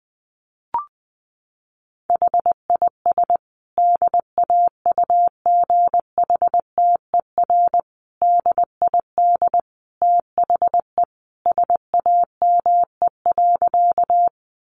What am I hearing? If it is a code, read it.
Morse code, heard as a keyed tone, about 20 wpm, 700 Hz. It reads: HIS DAUGHTER DID THE SAME.
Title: Sentences from Top 1500 Words 20wpm